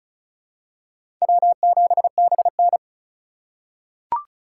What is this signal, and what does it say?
Morse code, heard as a keyed tone, about 36 wpm, 700 Hz. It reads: W7BN
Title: USA Call Signs - 3x Character Spacing - 35wpm